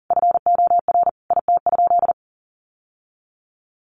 Morse code, heard as a keyed tone, about 37 wpm, 700 Hz. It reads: FOR IT?
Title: Sentences from Top 1200 Words 40wpm